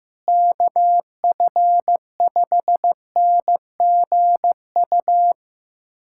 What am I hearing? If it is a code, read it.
Morse code, heard as a keyed tone, about 16 wpm, 700 Hz. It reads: KF5NGU
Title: USA Callsign DE Callsign - 6 Characters - 15wpm